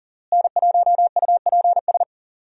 Morse code, heard as a keyed tone, about 40 wpm, 700 Hz. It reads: N1UPS
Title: USA Call Signs - 5 Characters - Repeated 3x - 40wpm